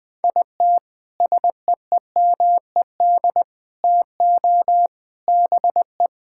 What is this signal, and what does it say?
Morse code, heard as a keyed tone, about 20 wpm, 700 Hz. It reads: IT SEEMED TO BE
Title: Sentences with 3-Word Phrases - 20wpm